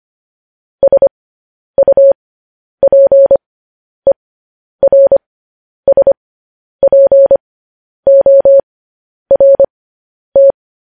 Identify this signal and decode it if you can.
Morse code, heard as a keyed tone, about 25 wpm, spaced as for 10 wpm, 550 Hz. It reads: SUPERSPORT
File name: podcast / News headlines-1x-25x10